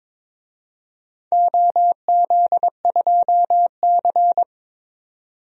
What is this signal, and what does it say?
Morse code, heard as a keyed tone, about 22 wpm, 700 Hz. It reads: OZ2C
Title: DXCC-100 Call Signs - 4 Characters - 22wpm